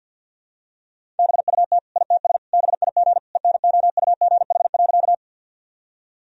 Morse code, heard as a keyed tone, about 48 wpm, 700 Hz. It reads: BUT ITS BIG ENOUGH.
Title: Sentences from Top 1000 Words - Review Part 1 - 50wpm